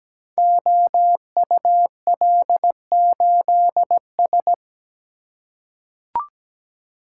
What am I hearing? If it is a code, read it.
Morse code, heard as a keyed tone, about 17 wpm, 700 Hz. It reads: OUL8S
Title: Code Groups - 5 Letter:Numbers - 17wpm